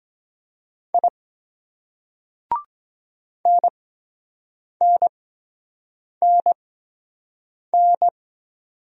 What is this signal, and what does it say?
Morse code, heard as a keyed tone, about 22 wpm, spaced as for 7 wpm, 700 Hz. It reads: I NNNN